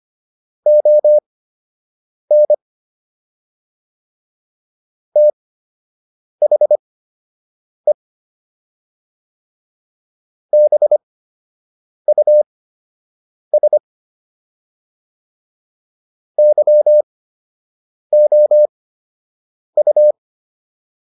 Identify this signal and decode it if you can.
Morse code, heard as a keyed tone, about 25 wpm, spaced as for 7 wpm, 600 Hz. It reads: ON THE BUS YOU